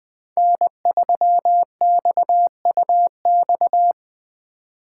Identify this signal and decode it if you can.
Morse code, heard as a keyed tone, about 24 wpm, 700 Hz. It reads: N3XUX
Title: Speed Racing - USA 1x3 Call Signs 20wpm